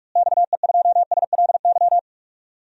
Morse code, heard as a keyed tone, about 44 wpm, 700 Hz. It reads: XE2SLY
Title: DXCC-100 Call Signs - 6 Characters - Repeated 3x - 45wpm